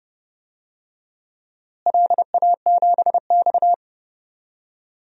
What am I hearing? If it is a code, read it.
Morse code, heard as a keyed tone, about 30 wpm, 700 Hz. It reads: LA7X
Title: DXCC-100 Call Signs - 4 Characters - Repeated 2x - 30wpm